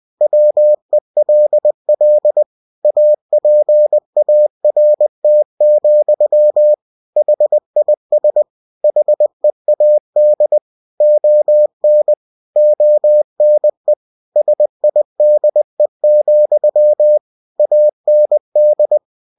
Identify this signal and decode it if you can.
Morse code, heard as a keyed tone, about 20 wpm, 600 Hz. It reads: WELL APART, HIS HEAD ON ONE SIDE, AND
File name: cw-books / winnie-the-pooh / winnie-20wpm-page-0026